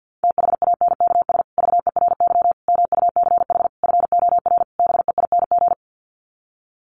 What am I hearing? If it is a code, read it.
Morse code, heard as a keyed tone, about 50 wpm, 700 Hz. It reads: THANKS VERY MUCH FOR BEING
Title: Sets of 5 Words Encore 50wpm